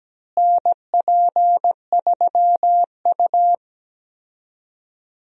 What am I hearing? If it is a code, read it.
Morse code, heard as a keyed tone, about 17 wpm, 700 Hz. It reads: NP3U
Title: USA Call Signs - 4 Characters - Repeated 2x - 17wpm